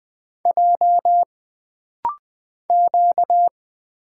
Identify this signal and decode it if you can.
Morse code, heard as a keyed tone, about 20 wpm, 700 Hz. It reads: J Q
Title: Single Character (Letter:Number:Punctuation:Prosign) - Rapid-Fire - 20wpm